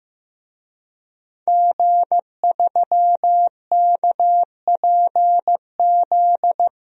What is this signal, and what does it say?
Morse code, heard as a keyed tone, about 15 wpm, 700 Hz. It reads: G3KPZ
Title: DXCC-100 Call Signs - 5 Characters - 15wpm